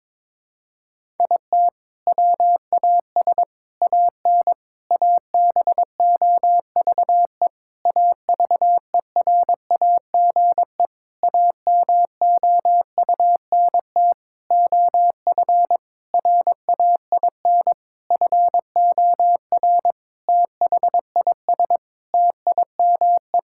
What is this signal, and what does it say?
Morse code, heard as a keyed tone, about 22 wpm, 700 Hz. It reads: IT WAS AN ABOVE AVERAGE AMOUNT OF RAIN FOR THIS TIME
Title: Sentences from Top 1000 Words - Review Part 6 - 22wpm